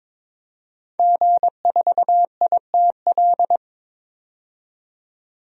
Morse code, heard as a keyed tone, about 22 wpm, 700 Hz. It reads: G4ITL
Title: DXCC-100 Call Signs - 5 Characters - 22wpm